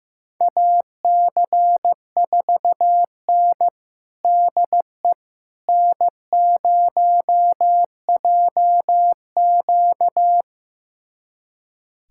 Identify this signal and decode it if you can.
Morse code, heard as a keyed tone, about 15 wpm, 700 Hz. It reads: AC4N DE N0JQ
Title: USA Callsign DE Callsign - 4 Characters - 15wpm